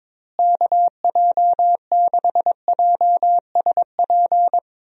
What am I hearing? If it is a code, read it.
Morse code, heard as a keyed tone, about 22 wpm, 700 Hz. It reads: KJ6JSP
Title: USA Call Signs - 6 Characters - Repeated 2x - 22wpm